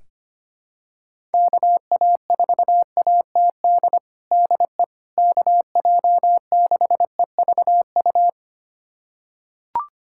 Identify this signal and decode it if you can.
Morse code, heard as a keyed tone, about 25 wpm, 700 Hz. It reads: KA4ATD DE KJ6EVU
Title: USA Callsign DE Callsign - 6 Characters - 25wpm